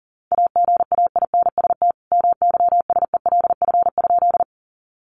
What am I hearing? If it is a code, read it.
Morse code, heard as a keyed tone, about 40 wpm, 700 Hz. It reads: AGAINST MYSELF?
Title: Sentences from Top 1300 Words 40wpm